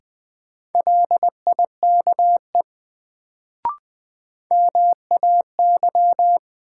Morse code, heard as a keyed tone, about 20 wpm, 700 Hz. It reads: LIKE MAY
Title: Sets of 2 Words - New Word Drill - 20wpm